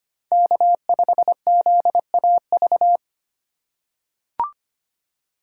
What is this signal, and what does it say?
Morse code, heard as a keyed tone, about 25 wpm, 700 Hz. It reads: K5ZAV
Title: USA Call Signs - 8x Character Spacing - 25wpm